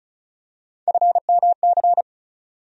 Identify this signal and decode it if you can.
Morse code, heard as a keyed tone, about 36 wpm, 700 Hz. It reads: FMC
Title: USA Call Sign Postfixes - 3 Letters 35wpm